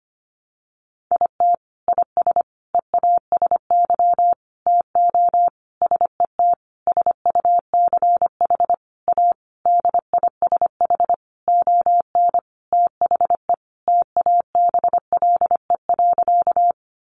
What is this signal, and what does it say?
Morse code, heard as a keyed tone, about 25 wpm, 700 Hz. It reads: IT IS EASY TO SET SUCH A DISH ON THE TABLE.
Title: Sentences from Top 1300 Words 25wpm